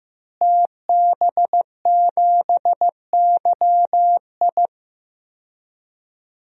Morse code, heard as a keyed tone, about 15 wpm, 700 Hz. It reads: TB7YI
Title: DXCC-100 Call Signs - 5 Characters - Repeated 2x - 15wpm